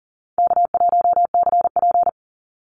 Morse code, heard as a keyed tone, about 40 wpm, 700 Hz. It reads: K1CP
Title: Easy Call Signs - 40wpm